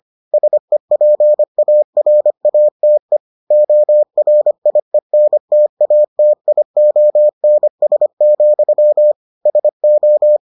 Morse code, heard as a keyed tone, about 25 wpm, 600 Hz. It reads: SEPARATE ORIENTATIONS, SO